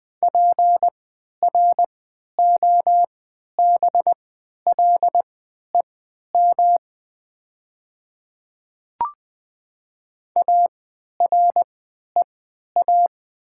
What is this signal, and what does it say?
Morse code, heard as a keyed tone, about 20 wpm, spaced as for 11 wpm, 700 Hz. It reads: PROBLEM AREA
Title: 3x Character Spacing - Top 500 Words - 20wpm